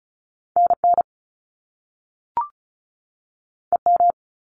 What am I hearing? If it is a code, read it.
Morse code, heard as a keyed tone, about 36 wpm, 700 Hz. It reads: NN EM
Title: Top Two Letter Combinations in Words 35wpm